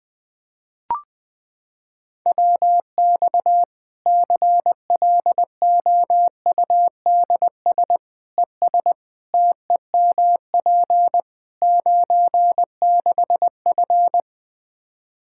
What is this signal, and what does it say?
Morse code, heard as a keyed tone, about 20 wpm, 700 Hz. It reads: WX CLOUDS ES TEMP 96F
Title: Practice QSO - Exchange 2 - Info - 20wpm